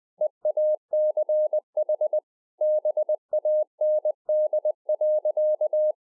Morse code, heard as a keyed tone, about 20 wpm, 610 Hz. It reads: EACH BAND.